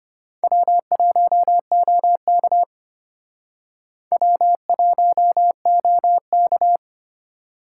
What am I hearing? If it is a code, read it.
Morse code, heard as a keyed tone, about 33 wpm, 700 Hz. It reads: W1OK W1OK